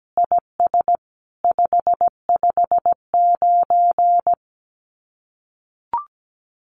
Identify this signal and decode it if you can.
Morse code, heard as a keyed tone, about 17 wpm, 700 Hz. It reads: IS 559
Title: Common QSO Phrases 17wpm